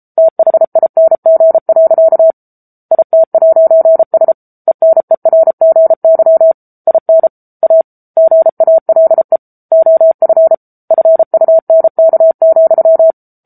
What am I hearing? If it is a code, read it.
Morse code, heard as a keyed tone, about 34 wpm, 650 Hz. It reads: THING. IT'S ENERGY IN A GALE OF FUNK,
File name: Book Two / 07-Chapter 7 - 33 WPM